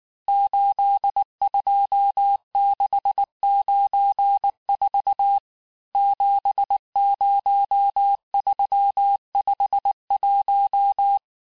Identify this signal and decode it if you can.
Morse code, heard as a keyed tone, about 19 wpm, 800 Hz. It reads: 82694 70351
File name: cours-du-lundi files / cours-1583--texte-chiffre-18-mpm